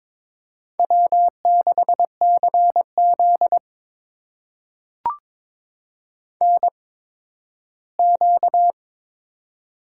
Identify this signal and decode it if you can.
Morse code, heard as a keyed tone, about 22 wpm, spaced as for 13 wpm, 700 Hz. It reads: W6CZ NQ
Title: USA Call Signs - 4 Characters - 8x Character Spacing - 22wpm